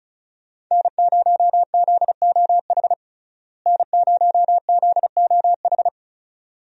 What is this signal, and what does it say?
Morse code, heard as a keyed tone, about 34 wpm, 700 Hz. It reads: N0ZOH N0ZOH
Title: USA Call Signs - 5 Characters - Repeated 2x - 35wpm